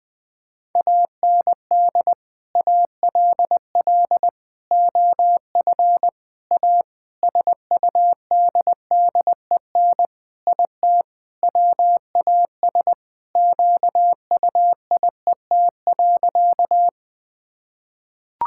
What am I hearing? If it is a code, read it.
Morse code, heard as a keyed tone, about 20 wpm, 700 Hz. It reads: AND ALL OF A SUDDEN IT WAS QUIET.
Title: Sentences with 5-Word Phrases - 20wpm